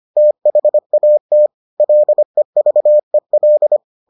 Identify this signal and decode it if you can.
Morse code, heard as a keyed tone, about 25 wpm, 600 Hz. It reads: THAT LEVEL